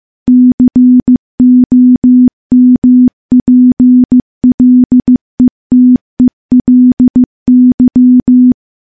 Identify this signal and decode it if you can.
Morse code, heard as a keyed tone, about 15 wpm, 250 Hz. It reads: COMPLETELY
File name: morse / morse-training-15WPM